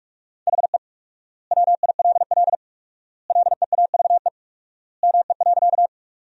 Morse code, heard as a keyed tone, about 46 wpm, 700 Hz. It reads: HE WILL LEAVE ME.
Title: 4x Word-Spacing - Sentences from Top 400 Words - 45wpm